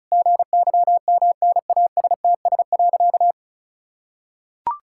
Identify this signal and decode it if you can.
Morse code, heard as a keyed tone, about 36 wpm, 700 Hz. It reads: GYMNASTS.
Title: Pangram Challenge v2 - 35wpm